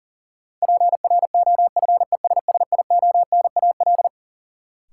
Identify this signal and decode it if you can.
Morse code, heard as a keyed tone, about 40 wpm, 700 Hz. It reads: PROFESSIONAL
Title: Sentences from Top 1000 Words - Review Part 4 - New Word Drill - 40wpm